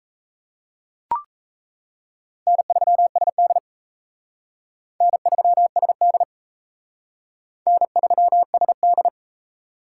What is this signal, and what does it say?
Morse code, heard as a keyed tone, about 34 wpm, 700 Hz. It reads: N3SD N3SD N3SD
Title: Speed Racing - USA 1x2 Call Signs 28wpm